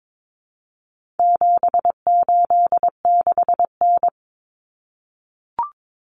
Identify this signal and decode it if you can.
Morse code, heard as a keyed tone, about 22 wpm, 700 Hz. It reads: 786N
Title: Common Cut Numbers - 4-Digit 22wpm